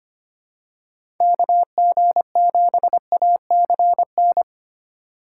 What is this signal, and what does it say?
Morse code, heard as a keyed tone, about 25 wpm, 700 Hz. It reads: KG7ACN
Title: USA Call Signs - 6 Characters - 3x Character Spacing - 25wpm